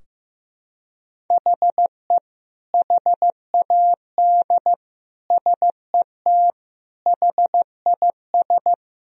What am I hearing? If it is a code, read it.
Morse code, heard as a keyed tone, about 15 wpm, 700 Hz. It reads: HE HAD SET HIS
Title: Sentences from Top 1000 Words - Review Part 2 - 15wpm